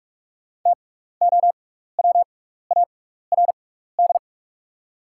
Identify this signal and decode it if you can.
Morse code, heard as a keyed tone, about 46 wpm, spaced as for 15 wpm, 700 Hz. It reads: TOWARD
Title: 6x Character Spacing - Top 500 Words - 45wpm